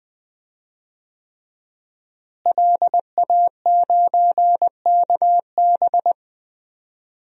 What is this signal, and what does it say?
Morse code, heard as a keyed tone, about 20 wpm, 700 Hz. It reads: LA9KB
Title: DXCC-100 Call Signs - 5 Characters - Repeated 3x - 20wpm